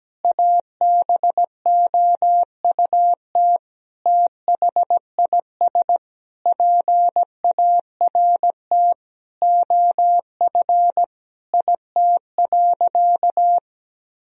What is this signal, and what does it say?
Morse code, heard as a keyed tone, about 17 wpm, 700 Hz. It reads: ABOUT THIS PART OF IT.
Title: Sentences from Top 1000 Words - Review Part 1 - 17wpm